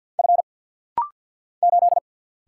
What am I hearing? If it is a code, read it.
Morse code, heard as a keyed tone, about 49 wpm, 700 Hz. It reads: F 8
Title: Single Letter-Number - Rapid-Fire 50wpm